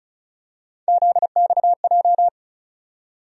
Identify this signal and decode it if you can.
Morse code, heard as a keyed tone, about 34 wpm, 700 Hz. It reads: ZXJ
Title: USA Call Sign Postfixes - 3 Letters 35wpm